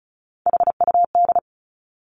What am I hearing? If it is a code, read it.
Morse code, heard as a keyed tone, about 36 wpm, 700 Hz. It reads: HUD
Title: Three Challenging Letter-Number Combinations 35wpm